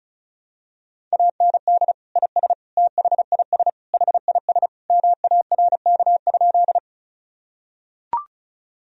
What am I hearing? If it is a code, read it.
Morse code, heard as a keyed tone, about 34 wpm, 700 Hz. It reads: AND IS THIS HIS MARK?
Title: Sentences from Top 1000 Words - Review Part 1 - 35wpm